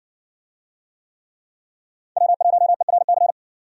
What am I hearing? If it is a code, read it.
Morse code, heard as a keyed tone, about 57 wpm, 700 Hz. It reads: W1ERC